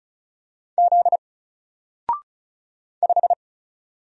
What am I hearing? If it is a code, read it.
Morse code, heard as a keyed tone, about 34 wpm, 700 Hz. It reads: Z 5
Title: Single Letter-Number Alt 35wpm